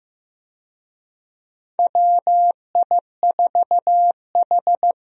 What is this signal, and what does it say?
Morse code, heard as a keyed tone, about 15 wpm, 700 Hz. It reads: WI4H
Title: USA Call Signs - 4 Characters - Repeated 2x - 15wpm